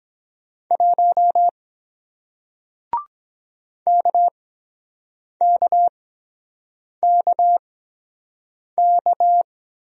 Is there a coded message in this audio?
Yes: Morse code from 0.7 to 9.4 s, heard as a keyed tone, about 23 wpm, spaced as for 7 wpm, 700 Hz. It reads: 1 KKKK